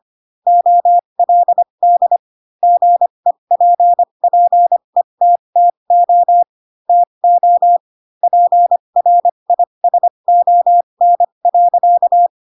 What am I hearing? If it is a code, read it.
Morse code, heard as a keyed tone, about 25 wpm, spaced as for 22 wpm, 700 Hz. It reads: OLD GEPPETTO TO PRISON.
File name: Pinocchio / Pinocchio ch4 18 25 ik1hge